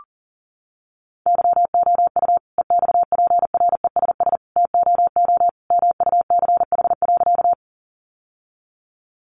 Morse code, heard as a keyed tone, about 40 wpm, 700 Hz. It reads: YOU EXPRESS TOO MUCH.
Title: Sentences from Top 1100 Words 40wpm